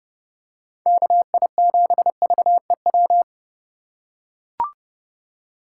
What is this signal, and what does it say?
Morse code, heard as a keyed tone, about 30 wpm, 700 Hz. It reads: KI7VEW
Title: USA Call Signs - 4x Character Spacing - 30wpm